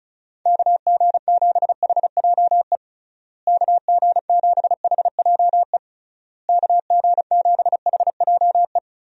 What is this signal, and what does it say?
Morse code, heard as a keyed tone, about 34 wpm, 700 Hz. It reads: KG7HJE KG7HJE KG7HJE
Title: USA Call Signs - 6 Characters - Repeated 3x - 35wpm